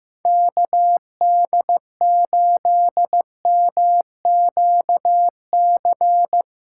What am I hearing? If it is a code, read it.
Morse code, heard as a keyed tone, about 15 wpm, 700 Hz. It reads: KD8MQC
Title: USA Callsign DE Callsign 15wpm